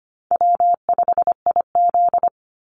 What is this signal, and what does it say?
Morse code, heard as a keyed tone, about 25 wpm, 700 Hz. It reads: W5IZ
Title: Call Signs - 25wpm